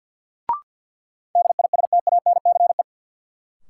Morse code, heard as a keyed tone, about 51 wpm, 700 Hz. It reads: DISTANCE
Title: Sentences from Top 1200 Words - New Word Drill - 50wpm